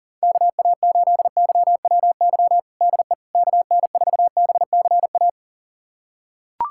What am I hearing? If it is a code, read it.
Morse code, heard as a keyed tone, about 40 wpm, 700 Hz. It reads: KA8YWY DE KN4BCA
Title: USA Callsign DE Callsign - 6 Characters - 40wpm